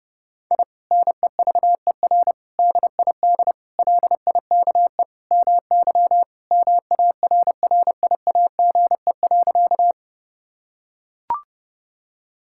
Morse code, heard as a keyed tone, about 32 wpm, 700 Hz. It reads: I NEVER DID LIKE MY MARRIAGE.
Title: Sentences from Top 1000 Words - Review Part 5 - 30wpm